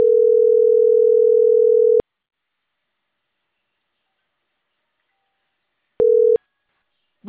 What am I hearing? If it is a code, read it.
Ringback tone (North America), 440 + 480 Hz.